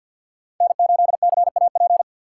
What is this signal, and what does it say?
Morse code, heard as a keyed tone, about 50 wpm, 700 Hz. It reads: N8CAP
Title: USA Call Signs - 4x Character Spacing - 50wpm